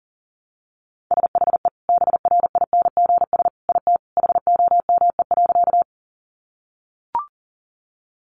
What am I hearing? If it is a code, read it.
Morse code, heard as a keyed tone, about 40 wpm, 700 Hz. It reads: SHE BRINGS IT HOME.
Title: Easy Phrases 40wpm